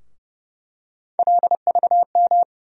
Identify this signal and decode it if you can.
Morse code, heard as a keyed tone, about 32 wpm, 700 Hz. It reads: LVM
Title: USA Call Sign Postfixes - 3 Letters 30wpm